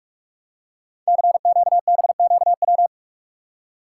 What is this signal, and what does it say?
Morse code, heard as a keyed tone, about 44 wpm, 700 Hz. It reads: CQBQW
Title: Code Groups - 5 Letter:Numbers - 45wpm